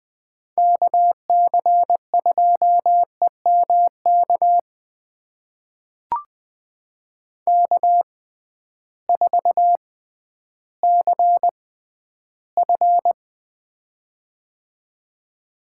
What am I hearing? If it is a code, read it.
Morse code, heard as a keyed tone, about 20 wpm, spaced as for 12 wpm, 700 Hz. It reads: KC2EMK K4CF
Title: USA Call Signs - 6x Character Spacing - 20wpm